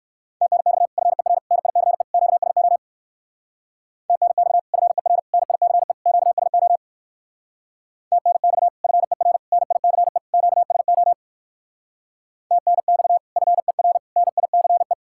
Morse code, heard as a keyed tone, about 58 wpm, 700 Hz. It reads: TNX FER NICE QSO TNX FER NICE QSO TNX FER NICE QSO TNX FER NICE